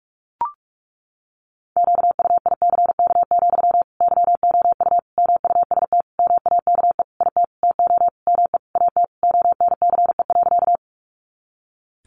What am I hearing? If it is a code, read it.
Morse code, heard as a keyed tone, about 44 wpm, 700 Hz. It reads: QUICK, YOU MUST MAKE IT TO ME AT ONCE.
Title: Sentences from Top 500 Words 45wpm